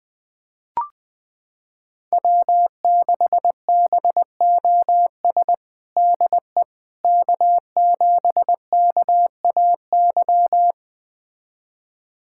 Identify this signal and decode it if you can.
Morse code, heard as a keyed tone, about 20 wpm, 700 Hz. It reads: W6BOS DE K7KAY
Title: USA Callsign DE Callsign - 5 Characters - 20wpm